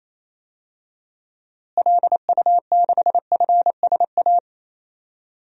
Morse code, heard as a keyed tone, about 28 wpm, 700 Hz. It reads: LU6FSA
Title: DXCC-100 Call Signs - 6 Characters - Repeated 3x - 28wpm